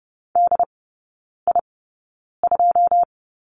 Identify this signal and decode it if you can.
Morse code, heard as a keyed tone, about 30 wpm, spaced as for 9 wpm, 700 Hz. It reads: DI2
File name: 081 Lesson 14e - Number 2 - Three Characters 30wpm